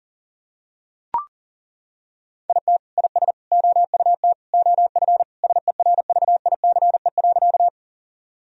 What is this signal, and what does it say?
Morse code, heard as a keyed tone, about 37 wpm, 700 Hz. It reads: IT IS OUT OF SERVICE.
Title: Sentences from Top 1000 Words - Review Part 4 - 40wpm